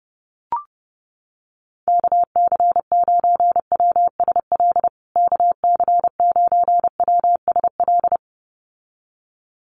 Morse code, heard as a keyed tone, about 30 wpm, 700 Hz. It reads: KC9WSL KC9WSL
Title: USA Call Signs Repeated 2x 30wpm